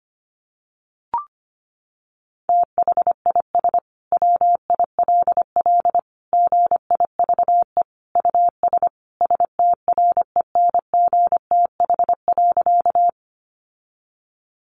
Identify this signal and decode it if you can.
Morse code, heard as a keyed tone, about 25 wpm, 700 Hz. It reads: THIS WILL GIVE US STRENGTH.
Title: Sentences from Top 1300 Words 25wpm